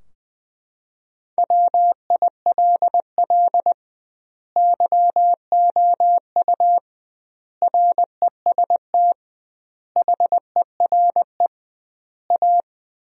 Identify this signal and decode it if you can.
Morse code, heard as a keyed tone, about 20 wpm, 700 Hz. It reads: WILL YOU REST HERE A
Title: 2x Word-Spacing - Sentences from Top 500 Words - 20wpm